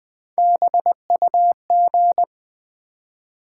Morse code, heard as a keyed tone, about 20 wpm, 700 Hz. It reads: BUG
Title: Build A Word - 1 Letter At A Time - 3-Letter Words - 20wpm